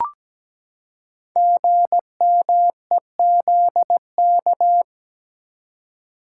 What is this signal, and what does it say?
Morse code, heard as a keyed tone, about 17 wpm, 700 Hz. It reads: GMEZK
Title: Code Groups - 5 Letter:Numbers - 17wpm